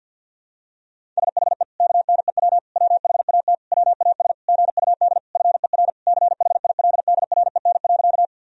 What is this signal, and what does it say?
Morse code, heard as a keyed tone, about 50 wpm, 700 Hz. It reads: SHE KNEW WHAT WAS GUD FER CHILDREN.
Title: Sentences from Top 400 Words and Abbreviations 50wpm